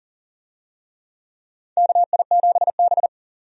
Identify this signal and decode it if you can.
Morse code, heard as a keyed tone, about 40 wpm, 700 Hz. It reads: KI7B
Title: USA Call Signs - 4 Characters - 40wpm